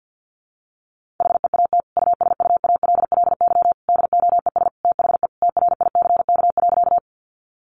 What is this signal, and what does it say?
Morse code, heard as a keyed tone, about 50 wpm, 700 Hz. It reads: HEAT USUALLY DOES THE TRICK.
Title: Sentences from Top 1500 Words 50wpm